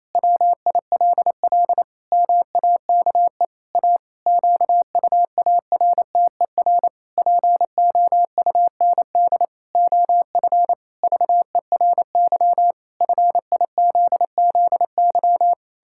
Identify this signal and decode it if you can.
Morse code, heard as a keyed tone, about 27 wpm, 700 Hz. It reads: WILL MAKE A QUARTER POUND OF VERY FIZZY
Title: Pangram Challenge v2 - 28wpm